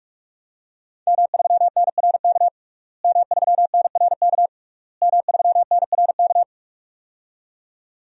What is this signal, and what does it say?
Morse code, heard as a keyed tone, about 46 wpm, 700 Hz. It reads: M3NRK M3NRK M3NRK
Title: DXCC-100 Call Signs - 5 Characters - Repeated 3x - 45wpm